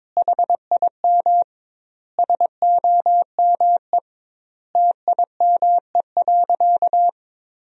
Morse code, heard as a keyed tone, about 22 wpm, 700 Hz. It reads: HIM SOME TIME.
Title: 2x Word-Spacing - Sentences from Top 300 Words - 22wpm